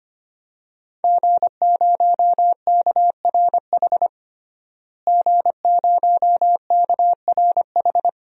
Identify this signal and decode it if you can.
Morse code, heard as a keyed tone, about 25 wpm, 700 Hz. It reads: G0KRH G0KRH
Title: DXCC-100 Call Signs - 5 Characters - Repeated 2x - 25wpm